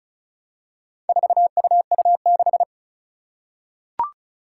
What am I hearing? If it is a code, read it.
Morse code, heard as a keyed tone, about 36 wpm, 700 Hz. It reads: 4UU6
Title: Four Challenging Letter-Number Combinations 35wpm